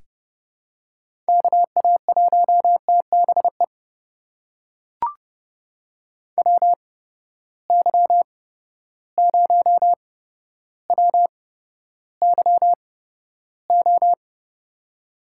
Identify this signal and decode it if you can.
Morse code, heard as a keyed tone, about 30 wpm, spaced as for 13 wpm, 700 Hz. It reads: KA1TBE WY0WYO
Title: USA Call Signs - 6 Characters - 8x Character Spacing - 30wpm